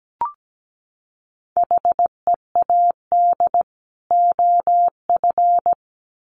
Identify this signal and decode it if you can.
Morse code, heard as a keyed tone, about 17 wpm, 700 Hz. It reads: HEAD OF
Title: Sets of 3 Words 17wpm